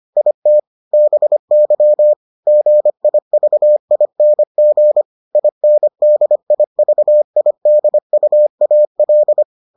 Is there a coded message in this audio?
Yes: Morse code from 0.2 to 9.4 s, heard as a keyed tone, about 25 wpm, 600 Hz. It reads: IT BY GIVING INDIVIDUAL